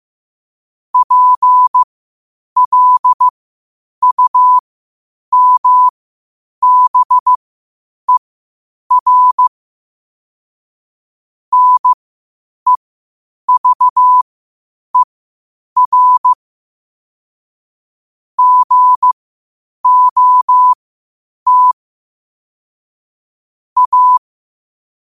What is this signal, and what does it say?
Morse code, heard as a keyed tone, about 15 wpm, spaced as for 8 wpm, 1000 Hz. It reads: PLUMBER NEVER GOT A
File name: qotdfiles / 8wpm